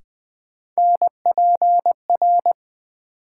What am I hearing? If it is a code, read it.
Morse code, heard as a keyed tone, about 20 wpm, 700 Hz. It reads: NPR
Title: Sets of 5 Words - New Word Drill - 20wpm